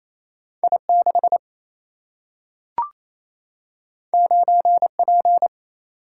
Morse code, heard as a keyed tone, about 28 wpm, 700 Hz. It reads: I6 9P
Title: Single-Letter Single-Number 28wpm